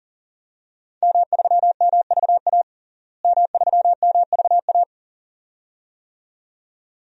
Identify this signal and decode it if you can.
Morse code, heard as a keyed tone, about 40 wpm, 700 Hz. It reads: M3MVA M3MVA
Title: DXCC-100 Call Signs - 5 Characters - Repeated 2x - 40wpm